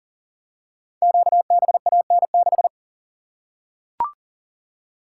Morse code, heard as a keyed tone, about 42 wpm, 690 Hz. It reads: QBAN6
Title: Code Groups - 5 Letter:Numbers - 40wpm